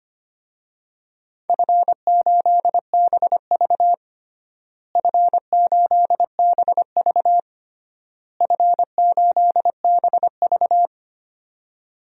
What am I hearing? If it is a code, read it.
Morse code, heard as a keyed tone, about 25 wpm, 700 Hz. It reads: F8BV F8BV F8BV
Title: DXCC-100 Call Signs - 4 Characters - Repeated 3x - 25wpm